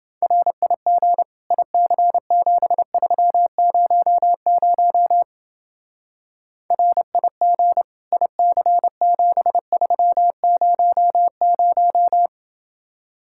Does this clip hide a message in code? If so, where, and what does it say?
Yes: Morse code from 0.2 to 12.3 s, heard as a keyed tone, about 28 wpm, 700 Hz. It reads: RIG IC7300 RIG IC7300